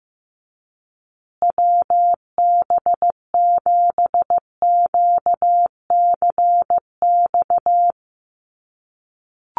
Morse code, heard as a keyed tone, about 15 wpm, 700 Hz. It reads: WB7QCX
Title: USA Call Signs Repeated 2x 15wpm